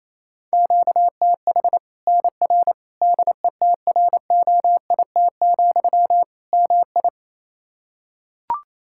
Morse code, heard as a keyed tone, about 27 wpm, 700 Hz. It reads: QTH NR DETROIT, MI
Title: Practice QSO - Exchange 1 - Beginning - 28wpm